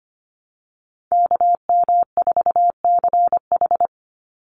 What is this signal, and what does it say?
Morse code, heard as a keyed tone, about 25 wpm, 700 Hz. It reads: KM4CH
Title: Call Signs Repeated 2x - 25wpm